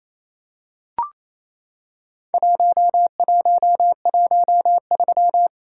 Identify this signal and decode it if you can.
Morse code, heard as a keyed tone, about 27 wpm, 700 Hz. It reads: 1113
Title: Worked All Europe DX Contest - QTC Exchange - 28wpm